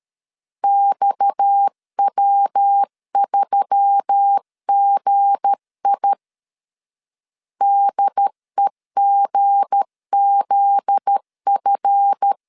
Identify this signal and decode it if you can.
Morse code, heard as a keyed tone, about 13 wpm, 800 Hz. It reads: XW3GI DEGZF
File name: LEZIONI / 9 286 12wpm 2305